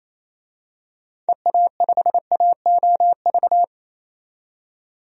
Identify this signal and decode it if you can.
Morse code, heard as a keyed tone, about 28 wpm, 700 Hz. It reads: EA5AOV
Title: DXCC-100 Call Signs - 6 Characters - 28wpm